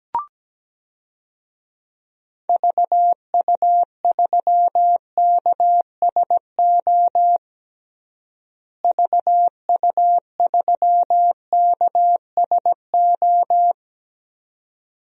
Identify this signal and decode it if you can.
Morse code, heard as a keyed tone, about 17 wpm, 700 Hz. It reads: VU3KSO VU3KSO
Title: DXCC-100 Call Signs - 6 Characters - Repeated 2x - 17wpm